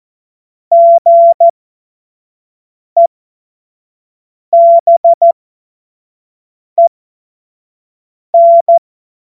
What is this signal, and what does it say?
Morse code, heard as a keyed tone, about 14 wpm, spaced as for 5 wpm, 680 Hz. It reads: GEBEN
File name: Morsekurs / Witze / Witz 08. November 2022 25 F70